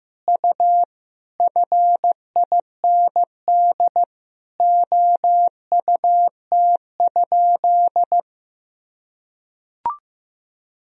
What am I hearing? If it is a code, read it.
Morse code, heard as a keyed tone, about 15 wpm, 700 Hz. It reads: U FIND OUT?
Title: Sentences from Top 200 Words and Abbreviations 15wpm